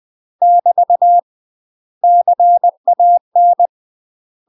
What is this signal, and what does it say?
Morse code, heard as a keyed tone, about 20 wpm, 700 Hz. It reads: <BT> CAN
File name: UK MASTER QSOs TRAINING / 20WPM / UK MASTER QSOs 0015